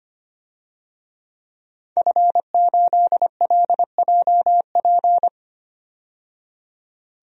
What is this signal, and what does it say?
Morse code, heard as a keyed tone, about 25 wpm, 700 Hz. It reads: F8LJP
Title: DXCC-100 Call Signs - 5 Characters - 25wpm